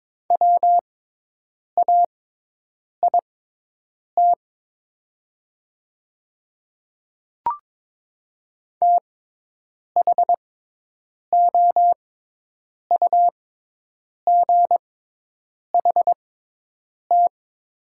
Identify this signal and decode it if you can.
Morse code, heard as a keyed tone, about 22 wpm, spaced as for 8 wpm, 700 Hz. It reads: WAIT THOUGHT
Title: 6x Character Spacing - Top 500 Words - 22wpm